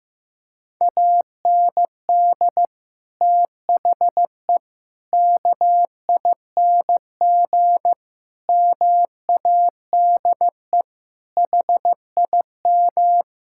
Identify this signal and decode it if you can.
Morse code, heard as a keyed tone, about 15 wpm, 700 Hz. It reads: AND THE KING MADE HIM
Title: Sentences from Top 1600 Words 15wpm